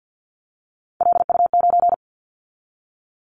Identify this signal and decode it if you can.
Morse code, heard as a keyed tone, about 50 wpm, 700 Hz. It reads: LU9
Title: Callsign Prefixes for Top 100 DXCC - 50wpm